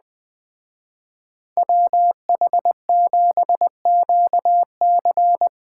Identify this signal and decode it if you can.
Morse code, heard as a keyed tone, about 20 wpm, 700 Hz. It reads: WH7QC
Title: Speed Racing - USA 2x2 Call Signs 17wpm